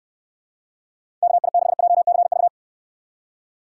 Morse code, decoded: KI6JCV

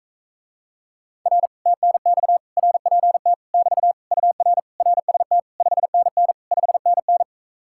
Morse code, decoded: R TNX RPT <BT> UR RST 5NN 5NN

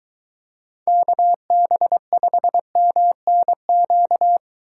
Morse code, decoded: KB5MNQ

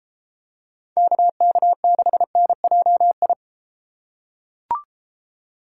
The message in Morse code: KK6NJI

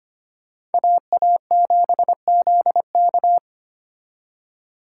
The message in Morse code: AA7ZK

